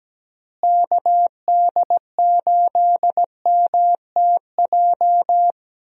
Morse code, heard as a keyed tone, about 17 wpm, 700 Hz. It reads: KD8MTJ
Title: USA Call Signs - 6 Characters - Repeated 3x - 17wpm